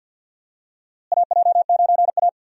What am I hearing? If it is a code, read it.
Morse code, heard as a keyed tone, about 46 wpm, 700 Hz. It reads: AJ9A
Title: USA Call Signs - 4 Characters - Repeated 3x - 50wpm